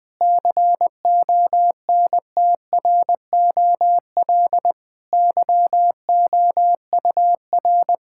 Morse code, heard as a keyed tone, about 20 wpm, 700 Hz. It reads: CONTROL YOUR
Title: Difficult Sentences with Hints - 20wpm